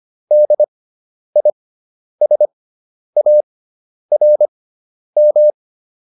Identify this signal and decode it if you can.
Morse code, heard as a keyed tone, about 25 wpm, spaced as for 10 wpm, 600 Hz. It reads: DISARM